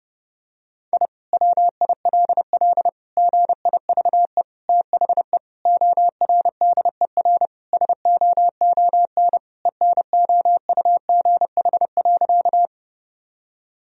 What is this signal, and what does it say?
Morse code, heard as a keyed tone, about 32 wpm, 700 Hz. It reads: I WILL GIVE THE ORDER SOON ENOUGH.
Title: Sentences from Top 1000 Words - Review Part 2 - 30wpm